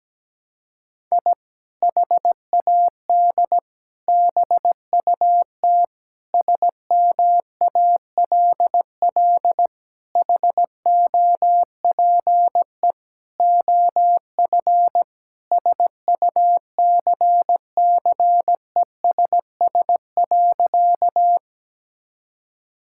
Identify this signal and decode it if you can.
Morse code, heard as a keyed tone, about 17 wpm, 700 Hz. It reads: I HAD BUT SMALL HOPE OF SUCCESS.
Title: Sentences from Top 1000 Words - Review Part 6 - 17wpm